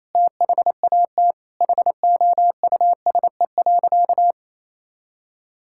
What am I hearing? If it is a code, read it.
Morse code, heard as a keyed tone, about 28 wpm, 700 Hz. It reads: THAT HOUSE.
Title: Sentences from Top 1000 Words - Review Part 5 - 28wpm